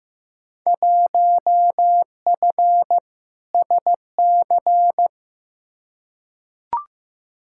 Morse code, heard as a keyed tone, about 15 wpm, 700 Hz. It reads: 1F SC
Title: Field Day Exchanges Encore - 15wpm